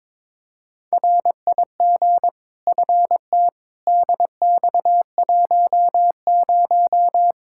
Morse code, heard as a keyed tone, about 22 wpm, 700 Hz. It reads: RIG FT DX10
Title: Practice QSO - Exchange 2 - Info - 22wpm